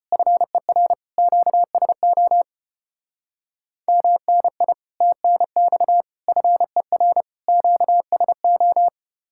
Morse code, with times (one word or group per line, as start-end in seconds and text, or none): 0.1-0.9 FER
1.2-2.4 QSO
3.9-4.7 MNI
5.0-6.0 TNX
6.3-7.2 FER
7.5-8.9 QSO